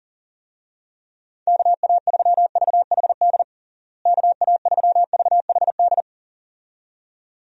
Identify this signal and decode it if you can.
Morse code, heard as a keyed tone, about 40 wpm, 700 Hz. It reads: KA3VHD KA3VHD
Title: USA Call Signs - 6 Characters - Repeated 2x - 40wpm